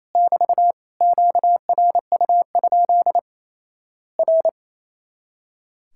Morse code, heard as a keyed tone, about 28 wpm, 700 Hz. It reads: <BT> QRU? R
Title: Worked All Europe DX Contest - QTC Exchange - 28wpm